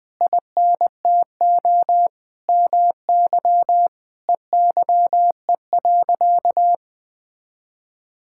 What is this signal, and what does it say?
Morse code, heard as a keyed tone, about 20 wpm, 700 Hz. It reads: INTO MY EYE.
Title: Sentences from Top 1600 Words 20wpm